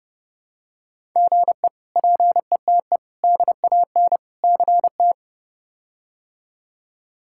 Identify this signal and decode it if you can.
Morse code, heard as a keyed tone, about 31 wpm, 700 Hz. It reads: GE PETE DAN CT
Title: K1USN SST QSO Practice - 30wpm